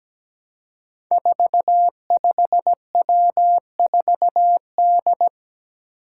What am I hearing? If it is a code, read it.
Morse code, heard as a keyed tone, about 17 wpm, 700 Hz. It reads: 45W4D
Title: Five Challenging Letter-Number Combinations 17wpm